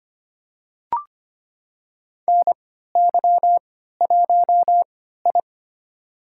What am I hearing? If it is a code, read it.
Morse code, heard as a keyed tone, about 25 wpm, spaced as for 14 wpm, 700 Hz. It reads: NY1I
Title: USA Call Signs - 4 Characters - 3x Character Spacing - 25wpm